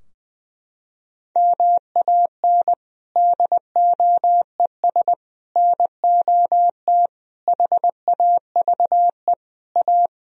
Morse code, heard as a keyed tone, about 20 wpm, 700 Hz. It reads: MAN DOES NOT HAVE A
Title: Sentences from Top 1000 Words - Review Part 3 - 20wpm